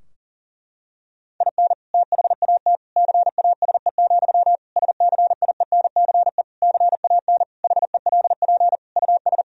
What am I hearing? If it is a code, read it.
Morse code, heard as a keyed tone, about 40 wpm, 700 Hz. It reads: IN THAT CASE, SCIENCE CAN HELP US
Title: Sentences from Top 1000 Words - Review Part 2 - 40wpm